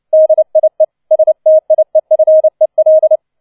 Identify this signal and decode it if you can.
Morse code, heard as a keyed tone, about 29 wpm, 630 Hz. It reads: DIE STIEFEL